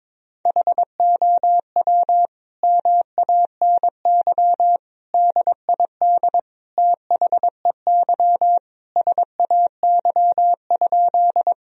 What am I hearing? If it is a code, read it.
Morse code, heard as a keyed tone, about 22 wpm, 700 Hz. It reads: HOW MANY DID THEY SAY?
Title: Sentences from Top 1000 Words - Review Part 1 - 22wpm